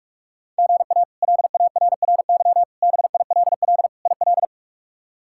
Morse code, heard as a keyed tone, about 46 wpm, 700 Hz. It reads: GA LARRY BILL IL
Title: K1USN SST QSO Practice - 45wpm